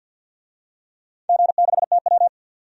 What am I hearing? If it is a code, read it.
Morse code, heard as a keyed tone, about 51 wpm, 700 Hz. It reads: G6TW
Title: DXCC-100 Call Signs - 4 Characters - 50wpm